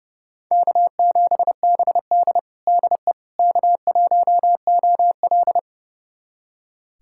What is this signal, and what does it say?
Morse code, heard as a keyed tone, about 30 wpm, 700 Hz. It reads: K7BD DE K1OL
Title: USA Callsign DE Callsign - 4 Characters - 30wpm